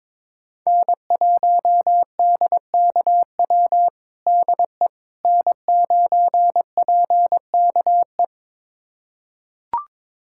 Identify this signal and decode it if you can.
Morse code, heard as a keyed tone, about 22 wpm, 700 Hz. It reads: N1DKW DE N9PKE
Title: USA Callsign DE Callsign - 5 Characters - 22wpm